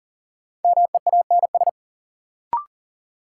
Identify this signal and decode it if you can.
Morse code, heard as a keyed tone, about 39 wpm, 700 Hz. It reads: MEANS
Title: Sets of 5 Words - New Word Drill - 40wpm